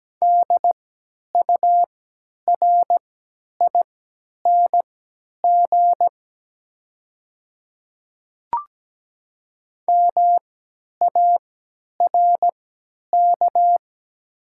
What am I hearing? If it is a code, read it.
Morse code, heard as a keyed tone, about 17 wpm, spaced as for 10 wpm, 700 Hz. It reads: DURING MARK
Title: 3x Character Spacing - Top 500 Words - 17wpm